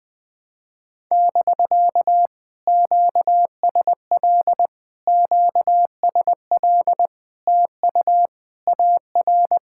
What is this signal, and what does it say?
Morse code, heard as a keyed tone, about 20 wpm, 700 Hz. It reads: <BK> QSL QSL TU AR
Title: POTA QSO Practice - 20wpm